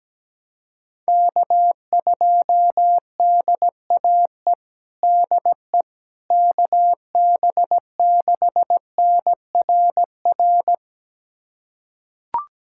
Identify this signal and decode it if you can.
Morse code, heard as a keyed tone, about 17 wpm, 700 Hz. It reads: K2DAE DE KB6NRR
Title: USA Callsign DE Callsign 17wpm